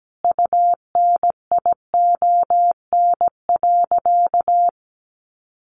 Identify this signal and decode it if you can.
Morse code, heard as a keyed tone, about 17 wpm, 700 Hz. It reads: UNION.
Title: Sentences from Top 1400 Words 17wpm